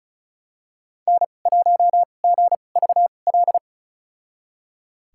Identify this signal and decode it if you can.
Morse code, heard as a keyed tone, about 36 wpm, spaced as for 26 wpm, 700 Hz. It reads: N1GVL
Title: USA Call Signs - 5 Characters - 2x Character Spacing - 35wpm